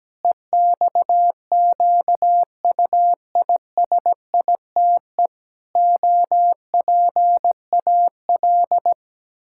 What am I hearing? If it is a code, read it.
Morse code, heard as a keyed tone, about 17 wpm, 700 Hz. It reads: EXQUISITE OPAL
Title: Pangram Challenge v2 - 17wpm